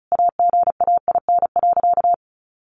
Morse code, heard as a keyed tone, about 34 wpm, 700 Hz. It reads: AGAIN.
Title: Sentences from Top 1500 Words 35wpm